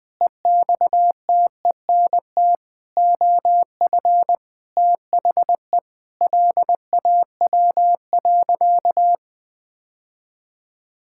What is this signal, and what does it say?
Morse code, heard as a keyed tone, about 19 wpm, 700 Hz. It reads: EXTENT OF THE LAW.
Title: Sentences with 7-Word Phrases - Spoken Beginning - 20wpm